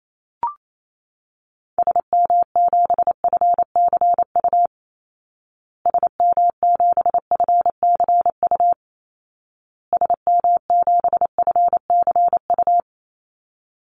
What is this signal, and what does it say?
Morse code, heard as a keyed tone, about 28 wpm, 700 Hz. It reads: SM7FCU SM7FCU SM7FCU
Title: Call Signs Repeated 2x - 28wpm